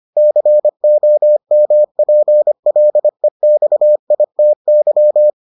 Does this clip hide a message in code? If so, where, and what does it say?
Yes: Morse code from 0.2 to 5.3 s, heard as a keyed tone, about 25 wpm, 600 Hz. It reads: COMPLEXITY